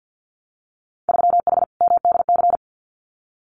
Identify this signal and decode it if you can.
Morse code, heard as a keyed tone, about 50 wpm, 700 Hz. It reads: 3H MDC
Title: Winter Field Day Exchanges Encore 50wpm